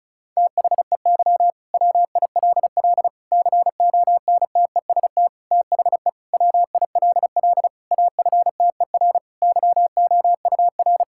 Morse code, heard as a keyed tone, about 36 wpm, 700 Hz. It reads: THEY WILL CONTEST THE WILL AFTER YOUR
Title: Sentences from Top 1600 Words 35wpm